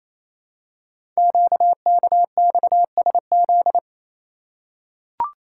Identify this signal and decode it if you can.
Morse code, heard as a keyed tone, about 27 wpm, 700 Hz. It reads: QKXSZ
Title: Code Groups - 5 Letters - 28wpm